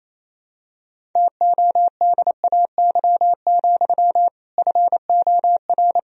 Morse code, heard as a keyed tone, about 28 wpm, 700 Hz. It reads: TODAY, FOR
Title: Sentences from Top 1000 Words - Review Part 1 - 28wpm